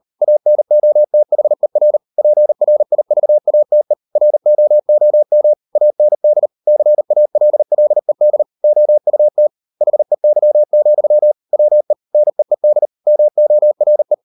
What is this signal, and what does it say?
Morse code, heard as a keyed tone, about 38 wpm, 600 Hz. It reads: ANOTHER PRIVATE ROOM AND CALLED OUT HEY, WE NEED MORE